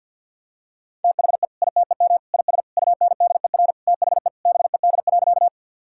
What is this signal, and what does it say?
Morse code, heard as a keyed tone, about 50 wpm, 700 Hz. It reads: THE ITEM IS UNDER THE BED.
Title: Sentences from Top 1000 Words - Review Part 6 - 50wpm